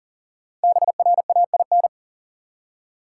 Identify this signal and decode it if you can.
Morse code, heard as a keyed tone, about 40 wpm, 700 Hz. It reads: BRAIN
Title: Five Letter Words - 40wpm